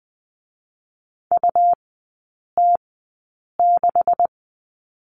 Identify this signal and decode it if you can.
Morse code, heard as a keyed tone, about 20 wpm, spaced as for 8 wpm, 700 Hz. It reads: UT6
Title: 152 Lesson 25e - Letter p - Three Characters 20wpm